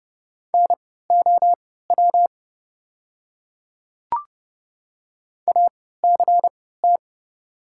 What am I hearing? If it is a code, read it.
Morse code, heard as a keyed tone, about 31 wpm, spaced as for 17 wpm, 700 Hz. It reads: NOW ACT
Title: 3x Character Spacing - Top 200 Words - 30wpm